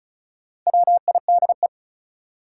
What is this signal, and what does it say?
Morse code, heard as a keyed tone, about 36 wpm, 700 Hz. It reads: WIDE
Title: Sets of 4 Words - New Word Drill - 35wpm